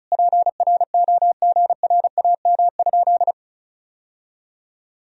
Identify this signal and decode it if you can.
Morse code, heard as a keyed tone, about 36 wpm, 700 Hz. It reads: PROGRAM?
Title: Sentences from Top 1000 Words - Review Part 4 - 35wpm